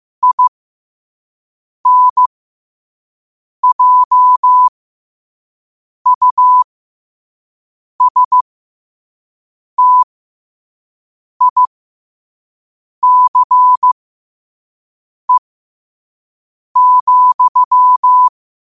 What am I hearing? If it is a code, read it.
Morse code, heard as a keyed tone, about 15 wpm, spaced as for 5 wpm, 1000 Hz. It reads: INJUSTICE,
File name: qotdfiles / 5wpm